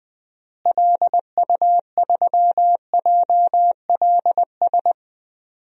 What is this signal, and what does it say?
Morse code, heard as a keyed tone, about 20 wpm, 700 Hz. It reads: LU3JLS
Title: DXCC-100 Call Signs - 6 Characters - Repeated 3x - 20wpm